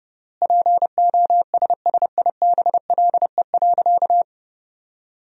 Morse code, heard as a keyed tone, about 30 wpm, 700 Hz. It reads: POSSIBLE.
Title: 3x Word-Spacing - Sentences from Top 500 Words - 30wpm